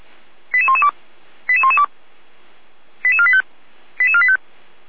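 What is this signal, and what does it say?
Five-tone selective call (ZVEI), heard as quick runs of five tones, tones of about 70 ms. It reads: 88112 88112 88485 88485
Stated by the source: ZVEI five-tone selective call received on an SDR.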